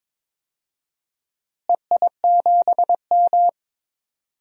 Morse code, heard as a keyed tone, about 22 wpm, 700 Hz. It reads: EI7M
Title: DXCC-100 Call Signs - 4 Characters - Repeated 2x - 22wpm